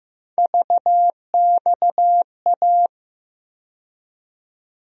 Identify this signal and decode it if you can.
Morse code, heard as a keyed tone, about 15 wpm, 700 Hz. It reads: VXA
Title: USA Call Sign Postfixes - 3 Letters 15wpm